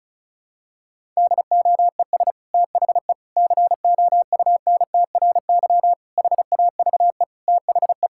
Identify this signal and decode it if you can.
Morse code, heard as a keyed tone, about 36 wpm, 700 Hz. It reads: DOES THE COUNTRY HAVE THE
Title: Sentences from Top 1000 Words - Review Part 5 - 35wpm